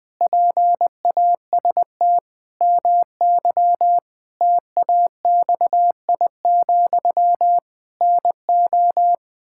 Morse code, heard as a keyed tone, about 20 wpm, 700 Hz. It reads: PAST MY TAXI, NO
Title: Pangram Challenge v2 - 20wpm